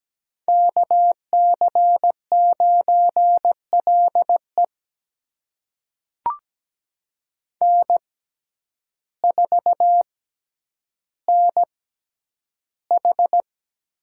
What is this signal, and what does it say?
Morse code, heard as a keyed tone, about 17 wpm, spaced as for 9 wpm, 700 Hz. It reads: KC9LE N4NH
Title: USA Call Signs - 6x Character Spacing - 17wpm